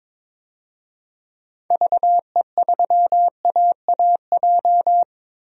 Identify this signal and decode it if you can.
Morse code, heard as a keyed tone, about 22 wpm, 700 Hz. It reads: VE3AAJ
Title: DXCC-100 Call Signs - 6 Characters - Repeated 3x - 22wpm